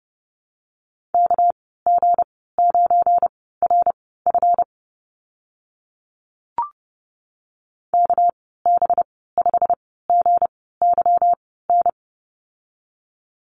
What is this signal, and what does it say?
Morse code, heard as a keyed tone, about 30 wpm, spaced as for 17 wpm, 700 Hz. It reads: KG9RF KB5GYN
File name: Lesson 37g - Letter x - Call Signs 30wpm